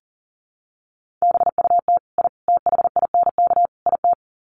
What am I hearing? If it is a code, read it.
Morse code, heard as a keyed tone, about 40 wpm, 700 Hz. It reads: BUT I THINK IT
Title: Sets of 4 Words 40wpm